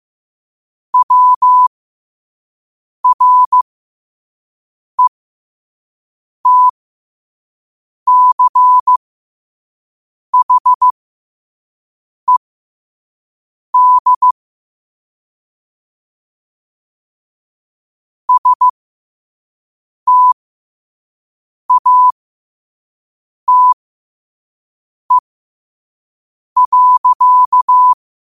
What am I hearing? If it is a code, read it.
Morse code, heard as a keyed tone, about 15 wpm, spaced as for 5 wpm, 1000 Hz. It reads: WRETCHED STATE.